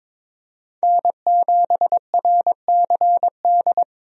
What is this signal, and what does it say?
Morse code, heard as a keyed tone, about 22 wpm, 700 Hz. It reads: N7RCD
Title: USA Call Signs - 5 Characters - Repeated 3x - 22wpm